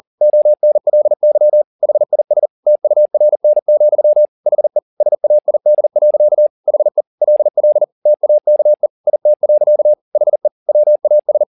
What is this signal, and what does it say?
Morse code, heard as a keyed tone, about 40 wpm, 600 Hz. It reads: ONLY HIS TURN, HE SAID. HE LL TAKE IT. HE WAS